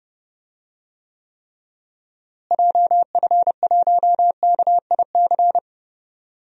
Morse code, heard as a keyed tone, about 30 wpm, 700 Hz. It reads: JF1KIC
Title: DXCC-100 Call Signs - 6 Characters - Repeated 3x - 30wpm